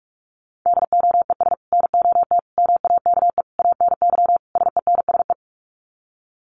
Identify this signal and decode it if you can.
Morse code, heard as a keyed tone, about 44 wpm, 700 Hz. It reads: DOES NOT MAKE ANY SENSE
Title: Sets of 5 Words Encore2 45wpm